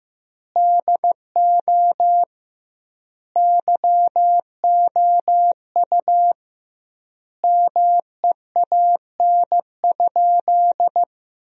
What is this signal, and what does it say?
Morse code, heard as a keyed tone, about 15 wpm, 700 Hz. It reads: DO YOU MEAN?
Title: 2x Word-Spacing - Sentences from Top 500 Words - 15wpm